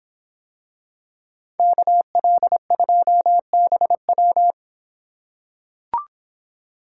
Morse code, heard as a keyed tone, about 26 wpm, 700 Hz. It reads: KL2BW